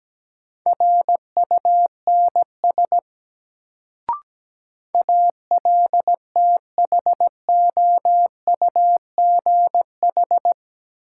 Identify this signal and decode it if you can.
Morse code, heard as a keyed tone, about 17 wpm, 700 Hz. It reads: RUNS ALTHOUGH
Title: Sentences from Top 600 Words - New Word Drill - 17wpm